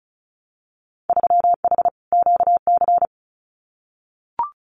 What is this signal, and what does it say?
Morse code, heard as a keyed tone, about 34 wpm, 700 Hz. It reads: 3H QC
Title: Winter Field Day Exchanges 35wpm